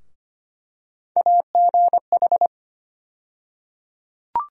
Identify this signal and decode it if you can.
Morse code, heard as a keyed tone, about 25 wpm, 700 Hz. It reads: AGH
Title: USA Call Sign Postfixes - 3 Letters 25wpm